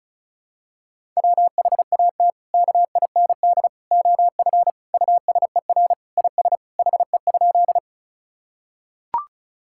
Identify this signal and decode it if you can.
Morse code, heard as a keyed tone, about 36 wpm, 700 Hz. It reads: WHAT KIND OF USER IS HE?
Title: Sentences from Top 1000 Words - Review Part 5 - 35wpm